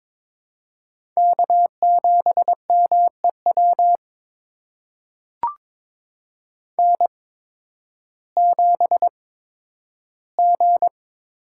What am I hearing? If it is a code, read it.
Morse code, heard as a keyed tone, about 22 wpm, spaced as for 12 wpm, 700 Hz. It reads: K7MEW N7G
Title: USA Call Signs - 5 Characters - 8x Character Spacing - 22wpm